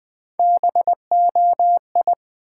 Morse code, heard as a keyed tone, about 20 wpm, 700 Hz. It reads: BOI
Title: Build A Word - 1 Letter At A Time - 4-Letter Words - 20wpm